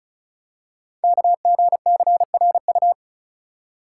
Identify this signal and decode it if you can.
Morse code, heard as a keyed tone, about 36 wpm, 690 Hz. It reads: KGCRU
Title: Code Groups - 5 Letters - 35wpm